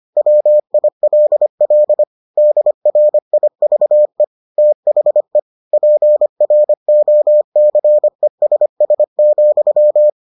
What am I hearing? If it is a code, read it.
Morse code, heard as a keyed tone, about 25 wpm, 600 Hz. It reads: WILL DRIVE THE PROCESS,